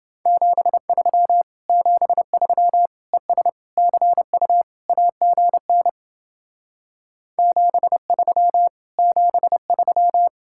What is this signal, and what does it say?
Morse code, heard as a keyed tone, about 29 wpm, 700 Hz. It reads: 73 73 ES CU AGN 73 73